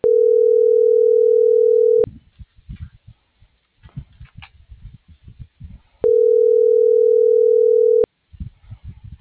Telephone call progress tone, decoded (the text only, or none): ringback tone